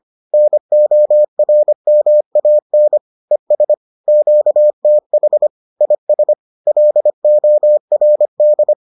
Morse code, heard as a keyed tone, about 25 wpm, 600 Hz. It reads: NORMAN ES QTH IS LORD